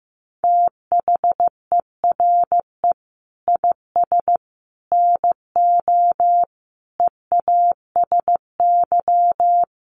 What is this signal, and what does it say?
Morse code, heard as a keyed tone, about 15 wpm, 700 Hz. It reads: THERE IS NO EASY
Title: Sentences from Top 1300 Words 15wpm